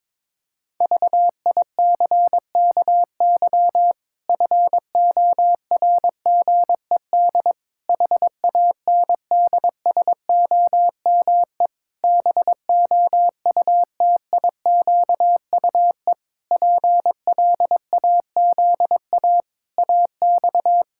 Morse code, heard as a keyed tone, about 22 wpm, 700 Hz. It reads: VICKY FORGED HANDSOME BOUTIQUE PLAZA AX
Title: Pangram Challenge v2 - 22wpm